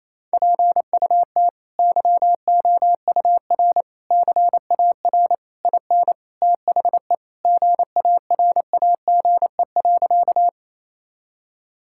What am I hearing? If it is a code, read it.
Morse code, heard as a keyed tone, about 28 wpm, 700 Hz. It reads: PUT YOUR CAR IN THE GARAGE.
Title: Sentences from Top 1600 Words 28wpm